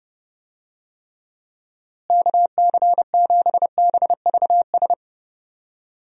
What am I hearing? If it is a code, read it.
Morse code, heard as a keyed tone, about 30 wpm, 700 Hz. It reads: KC7BVS
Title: USA Call Signs - 6 Characters - Repeated 2x - 30wpm